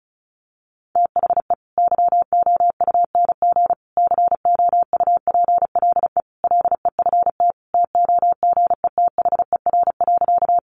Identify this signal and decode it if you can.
Morse code, heard as a keyed tone, about 34 wpm, 700 Hz. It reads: THE YOUNG COUPLE LEFT TOGETHER.
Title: Sentences from Top 1100 Words 35wpm